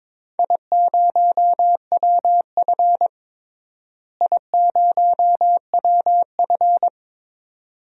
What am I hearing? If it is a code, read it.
Morse code, heard as a keyed tone, about 22 wpm, 700 Hz. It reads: I0WF I0WF
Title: DXCC-100 Call Signs - 4 Characters - Repeated 2x - 22wpm